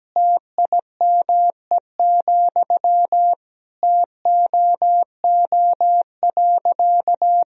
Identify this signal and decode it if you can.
Morse code, heard as a keyed tone, about 17 wpm, 700 Hz. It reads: TIME, TOO.
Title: Sentences from Top 1000 Words - Review Part 1 - 17wpm